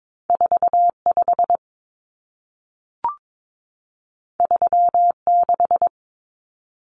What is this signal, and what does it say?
Morse code, heard as a keyed tone, about 22 wpm, 700 Hz. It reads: 45 36
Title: Common Cut Numbers - 2-Digit 22wpm